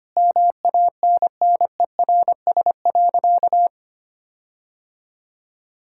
Morse code, heard as a keyed tone, about 25 wpm, 700 Hz. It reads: MANNERS.
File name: practice / 25wpm / Difficult Sentences with Hints - 25wpm